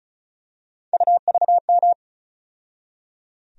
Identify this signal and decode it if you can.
Morse code, heard as a keyed tone, about 36 wpm, 700 Hz. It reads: UVM
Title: USA Call Sign Postfixes - 3 Letters 35wpm